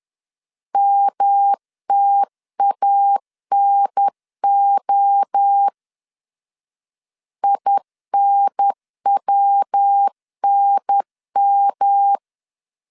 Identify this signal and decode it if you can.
Morse code, heard as a keyed tone, about 10 wpm, 800 Hz. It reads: MTANO INWNM
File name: CW / DET / 1 3 1500C S B 10wpm 3335